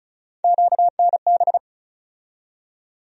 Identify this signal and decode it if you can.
Morse code, heard as a keyed tone, about 36 wpm, 700 Hz. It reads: QNB
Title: USA Call Sign Postfixes - 3 Letters 35wpm